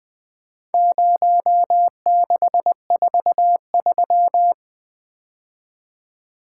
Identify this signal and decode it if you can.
Morse code, heard as a keyed tone, about 22 wpm, 700 Hz. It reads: 0643
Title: Speed-Racing - Four Letter-Number Combinations - 20wpm